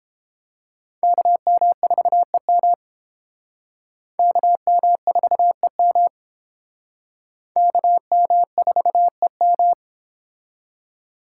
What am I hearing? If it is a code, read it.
Morse code, heard as a keyed tone, about 32 wpm, 700 Hz. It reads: KM4EM KM4EM KM4EM